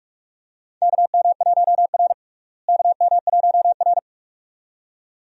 Morse code, heard as a keyed tone, about 46 wpm, 700 Hz. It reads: KM1R KM1R
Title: USA Call Signs - 4 Characters - Repeated 2x - 45wpm